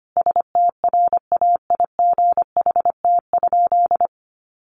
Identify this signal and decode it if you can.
Morse code, heard as a keyed tone, about 25 wpm, 700 Hz. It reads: STRAIGHT?
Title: Sentences from Top 1000 Words 25wpm